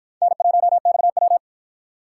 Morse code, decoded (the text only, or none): N1XW